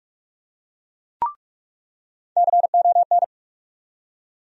Morse code, heard as a keyed tone, about 43 wpm, 700 Hz. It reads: CON
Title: USA Call Sign Postfixes - 3 Letters 45wpm